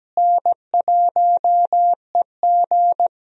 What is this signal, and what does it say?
Morse code, heard as a keyed tone, about 17 wpm, 700 Hz. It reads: N1EG
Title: USA Call Signs - 4 Characters - Repeated 3x - 17wpm